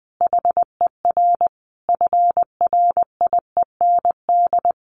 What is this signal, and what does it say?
Morse code, heard as a keyed tone, about 20 wpm, 700 Hz. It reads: HER FRIEND
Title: Sentences from Top 1500 Words 20wpm